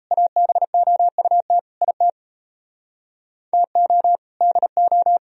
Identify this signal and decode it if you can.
Morse code, heard as a keyed tone, about 32 wpm, 700 Hz. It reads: ABOUT IT TO DO
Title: Speed-Racing - Sets of 5 Words 28wpm